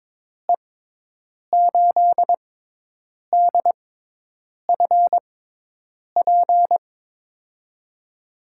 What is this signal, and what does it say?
Morse code, heard as a keyed tone, about 22 wpm, spaced as for 8 wpm, 700 Hz. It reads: E8DFP
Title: USA Call Signs - 6x Character Spacing - 22wpm